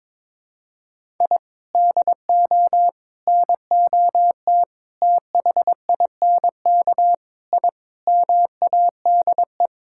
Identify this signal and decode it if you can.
Morse code, heard as a keyed tone, about 22 wpm, 700 Hz. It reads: I DO NOT THINK I MADE
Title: Sentences from Top 1000 Words - Review Part 6 - 22wpm